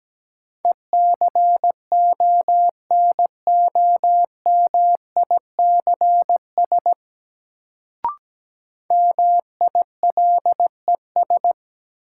Morse code, heard as a keyed tone, about 17 wpm, 700 Hz. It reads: ECONOMICS MILES
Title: Sentences from Top 1000 Words - Review Part 3 - New Word Drill - 17wpm